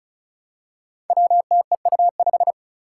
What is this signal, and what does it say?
Morse code, heard as a keyed tone, about 36 wpm, 700 Hz. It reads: WTEU5
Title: Five Challenging Letter-Number Combinations 35wpm